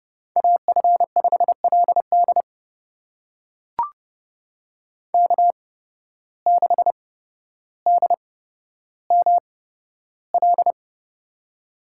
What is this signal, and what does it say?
Morse code, heard as a keyed tone, about 30 wpm, spaced as for 13 wpm, 700 Hz. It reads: AF5LD K6DML
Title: USA Call Signs - 5 Characters - 8x Character Spacing - 30wpm